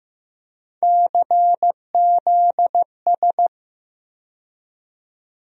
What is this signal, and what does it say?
Morse code, heard as a keyed tone, about 15 wpm, 700 Hz. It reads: CZS
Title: USA Call Sign Postfixes - 3 Letters 15wpm